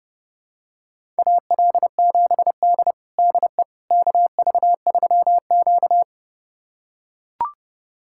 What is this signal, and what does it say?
Morse code, heard as a keyed tone, about 30 wpm, 700 Hz. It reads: AL7D DE KV3Q
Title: USA Callsign DE Callsign - 4 Characters - 30wpm